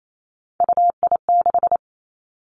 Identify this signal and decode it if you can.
Morse code, heard as a keyed tone, about 28 wpm, 700 Hz. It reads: UI6
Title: Three Challenging Letter-Number Combinations 28wpm